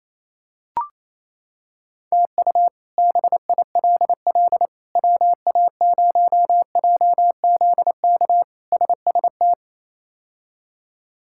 Morse code, heard as a keyed tone, about 28 wpm, 700 Hz. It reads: TU BILL WA0JZK SST
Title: K1USN SST QSO Practice - 28wpm